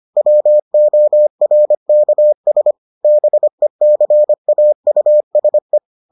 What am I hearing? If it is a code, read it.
Morse code, heard as a keyed tone, about 25 wpm, 600 Hz. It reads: WORKS BECAUSE